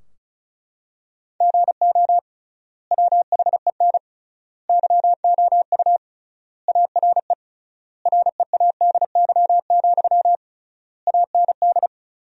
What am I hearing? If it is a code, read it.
Morse code, heard as a keyed tone, about 34 wpm, 700 Hz. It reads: GO WHEN YOU ARE READY, AND
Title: 3x Word-Spacing - Sentences from Top 500 Words - 35wpm